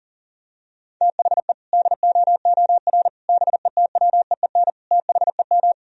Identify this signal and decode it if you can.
Morse code, heard as a keyed tone, about 40 wpm, 700 Hz. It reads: THE DOOR BETWEEN THEM
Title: Sentences from Top 1000 Words - Review Part 1 - 40wpm